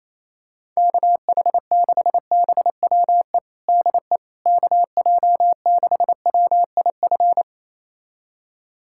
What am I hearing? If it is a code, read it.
Morse code, heard as a keyed tone, about 28 wpm, 700 Hz. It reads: KH6BWE DE KJ6WIF
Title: USA Callsign DE Callsign 28wpm